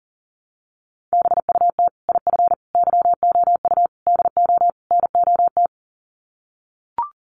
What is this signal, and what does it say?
Morse code, heard as a keyed tone, about 40 wpm, 700 Hz. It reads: BUT IF YOU DO NOT
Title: Sets of 5 Words Encore 40wpm